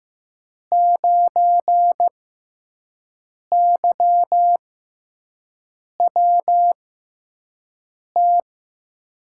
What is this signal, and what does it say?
Morse code, heard as a keyed tone, about 15 wpm, spaced as for 5 wpm, 700 Hz. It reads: 9YWT
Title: USA Call Signs - 6 Characters - 6x Character Spacing - 15wpm